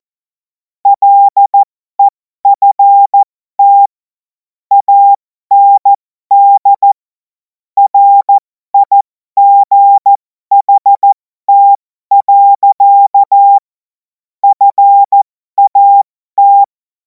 Morse code, heard as a keyed tone, about 14 wpm, spaced as for 12 wpm, 800 Hz. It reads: LEFT AND RIGHT. FAT
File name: Morsekurs / Witze